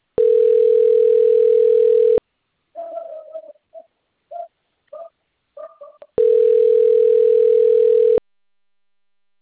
Ringback tone (North America), 440 + 480 Hz.
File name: rg-1600-unknown- - - .33856